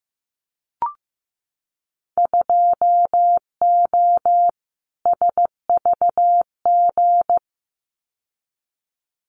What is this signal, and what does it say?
Morse code, heard as a keyed tone, about 15 wpm, 700 Hz. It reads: 2O SVG
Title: Winter Field Day Exchanges 15wpm